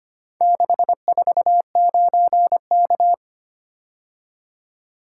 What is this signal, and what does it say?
Morse code, heard as a keyed tone, about 25 wpm, 700 Hz. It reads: 649K
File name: practice / 25wpm / Four Letter-Number Combinations - 25wpm